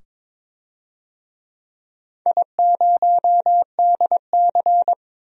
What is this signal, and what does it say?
Morse code, heard as a keyed tone, about 22 wpm, 700 Hz. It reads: I0DC
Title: DXCC-100 Call Signs - 4 Characters - 22wpm